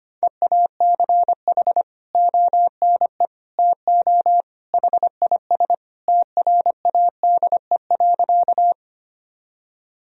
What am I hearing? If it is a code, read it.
Morse code, heard as a keyed tone, about 25 wpm, 700 Hz. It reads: EACH ONE TO HIS TRADE.
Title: Sentences from Top 1000 Words - Review Part 4 - 25wpm